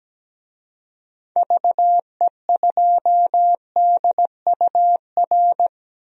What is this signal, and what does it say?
Morse code, heard as a keyed tone, about 17 wpm, 700 Hz. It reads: VE2DUR
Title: DXCC-100 Call Signs - 6 Characters - 17wpm